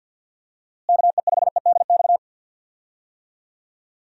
Morse code, heard as a keyed tone, about 46 wpm, 700 Hz. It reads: KE5EDX
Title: USA Call Signs - 6 Characters - Repeated 2x - 50wpm